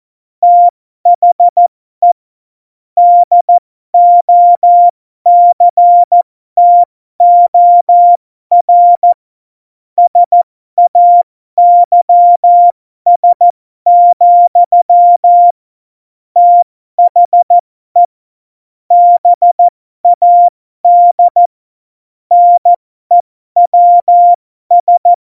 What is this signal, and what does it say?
Morse code, heard as a keyed tone, about 14 wpm, spaced as for 12 wpm, 700 Hz. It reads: THE DOCTOR SAYS, THE BAD NEWS